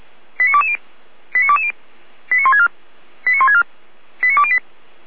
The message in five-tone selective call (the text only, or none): 89220 89220 89164 89164 89208